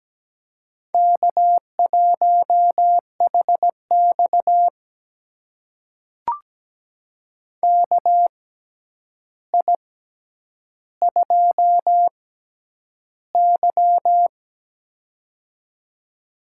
Morse code, heard as a keyed tone, about 17 wpm, spaced as for 9 wpm, 700 Hz. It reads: K1HX KI2Y
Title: USA Call Signs - 4 Characters - 6x Character Spacing - 17wpm